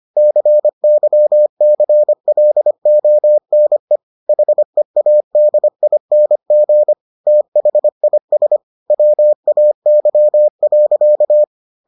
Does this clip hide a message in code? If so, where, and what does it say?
Yes: Morse code from 0.2 to 11.4 s, heard as a keyed tone, about 25 wpm, 600 Hz. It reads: CYCLONE HEADING THIS WAY.